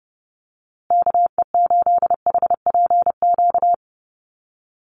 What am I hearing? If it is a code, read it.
Morse code, heard as a keyed tone, about 30 wpm, 700 Hz. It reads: KE8HPQ
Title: USA Call Signs Repeated 3x 30wpm